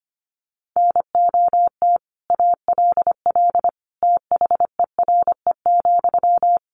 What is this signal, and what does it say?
Morse code, heard as a keyed tone, about 25 wpm, 700 Hz. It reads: NOT ALL THERE,
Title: Sentences from Top 600 Words 25wpm